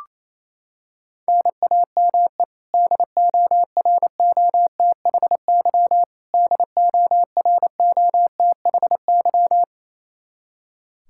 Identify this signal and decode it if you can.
Morse code, heard as a keyed tone, about 28 wpm, 700 Hz. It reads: NAME DOROTHY DOROTHY
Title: Practice QSO - Exchange 1 - Beginning - 28wpm